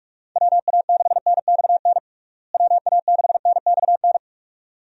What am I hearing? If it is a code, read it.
Morse code, heard as a keyed tone, about 46 wpm, 700 Hz. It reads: WA6NXN WA6NXN
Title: USA Call Signs - 6 Characters - Repeated 2x - 45wpm